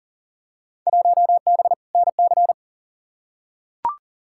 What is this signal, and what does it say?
Morse code, heard as a keyed tone, about 40 wpm, 700 Hz. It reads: 1B NC
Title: Field Day Exchanges - 40wpm